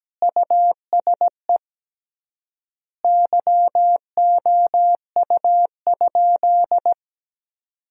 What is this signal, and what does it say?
Morse code, heard as a keyed tone, about 17 wpm, 700 Hz. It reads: USE YOU?
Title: 3x Word-Spacing - Sentences from Top 200 Words - 17wpm